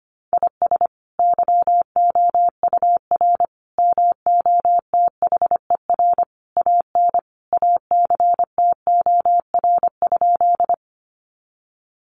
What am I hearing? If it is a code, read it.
Morse code, heard as a keyed tone, about 25 wpm, 700 Hz. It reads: IS YOUR MOTHER AN ACTOR?
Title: Sentences from Top 1500 Words 25wpm